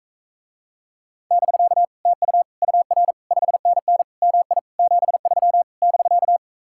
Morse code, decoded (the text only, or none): <BK> TU UR 5NN MI 73 <BK>